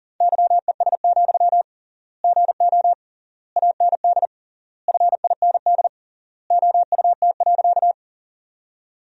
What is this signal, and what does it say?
Morse code, heard as a keyed tone, about 40 wpm, 700 Hz. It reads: YES, GO AND FIND OUT.
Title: 3x Word-Spacing - Sentences from Top 500 Words - 40wpm